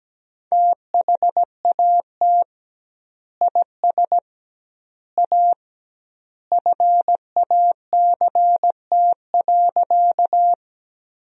Morse code, heard as a keyed tone, about 17 wpm, 700 Hz. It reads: THAT IS A FACT.
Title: 2x Word-Spacing - Sentences from Top 500 Words - 17wpm